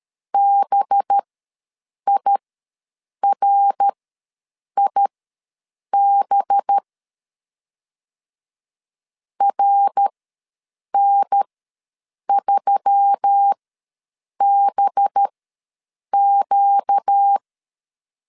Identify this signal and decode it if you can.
Morse code, heard as a keyed tone, about 13 wpm, spaced as for 7 wpm, 800 Hz. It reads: BIRIB RN3BQ
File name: LEZIONI / 5 LBVC / 5 LB ws 12wpm 3034